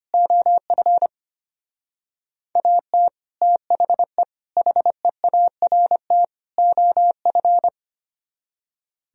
Morse code, heard as a keyed tone, about 33 wpm, 700 Hz. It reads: OF AT THE HEART OF